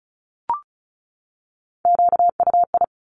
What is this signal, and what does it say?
Morse code, heard as a keyed tone, about 34 wpm, 700 Hz. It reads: QUI
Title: Top Three Letter Combinations in Words 35wpm